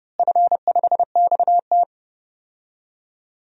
Morse code, heard as a keyed tone, about 30 wpm, 700 Hz. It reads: F5XT
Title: DXCC-100 Call Signs - 4 Characters - Repeated 3x - 30wpm